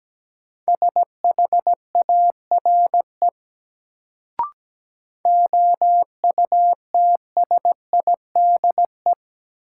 Morse code, heard as a keyed tone, about 17 wpm, 700 Hz. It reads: SHARE OUTSIDE
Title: Sentences from Top 900 Words - New Word Drill - 17wpm